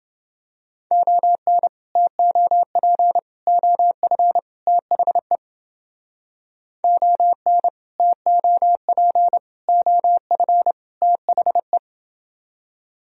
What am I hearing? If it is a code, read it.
Morse code, heard as a keyed tone, about 29 wpm, 700 Hz. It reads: ON TOP OF THE ON TOP OF THE